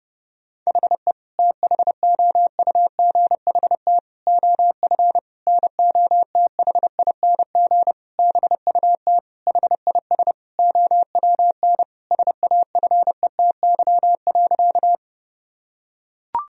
Morse code, heard as a keyed tone, about 30 wpm, 700 Hz. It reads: HE THOUGHT OF NOTHING BUT HIS OWN SAFETY.
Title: Sentences from Top 1000 Words - Review Part 4 - 30wpm